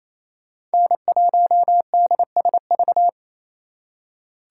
Morse code, heard as a keyed tone, about 33 wpm, 700 Hz. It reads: N1DSV
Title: Speed Racing - USA 1x3 Call Signs 28wpm